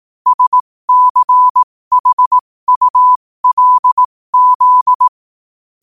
Morse code, heard as a keyed tone, about 18 wpm, spaced as for 15 wpm, 1000 Hz. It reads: SCHULZ
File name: qotdfiles / 15wpm